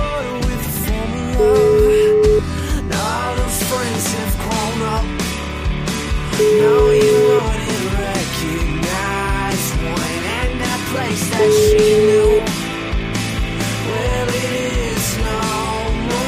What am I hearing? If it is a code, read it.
Ringback tone (Europe, 425 Hz), with noise.